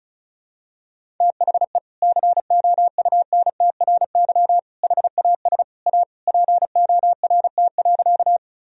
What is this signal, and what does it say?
Morse code, heard as a keyed tone, about 34 wpm, 700 Hz. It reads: THE COUNTRY HAS A PORT.
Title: Sentences from Top 1000 Words - Review Part 1 - 35wpm